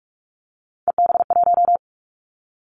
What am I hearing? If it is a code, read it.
Morse code, heard as a keyed tone, about 46 wpm, 700 Hz. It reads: EB1
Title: Callsign Prefixes for Top 25 DXCC - 45wpm